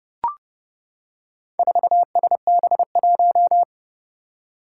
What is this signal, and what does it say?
Morse code, heard as a keyed tone, about 30 wpm, 700 Hz. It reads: 4SB1
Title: Four Challenging Letter-Number Combinations 30wpm